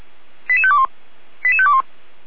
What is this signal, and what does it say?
Five-tone selective call (ZVEI), heard as quick runs of five tones, tones of about 70 ms. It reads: 88521 88521